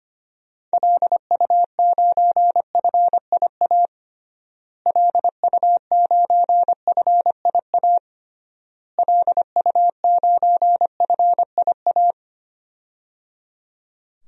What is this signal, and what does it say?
Morse code, heard as a keyed tone, about 25 wpm, 700 Hz. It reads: LU9FIA LU9FIA LU9FIA
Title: DXCC-100 Call Signs - 6 Characters - Repeated 3x - 25wpm